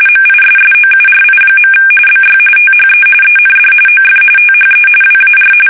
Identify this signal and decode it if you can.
RTTY, about 45.45 baud. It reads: MISSOURI. BY THE END OF THE WAR
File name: fsk850